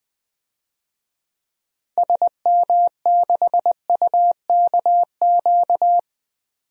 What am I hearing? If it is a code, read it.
Morse code, heard as a keyed tone, about 20 wpm, 700 Hz. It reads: SM6UKQ
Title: DXCC-100 Call Signs - 6 Characters - 20wpm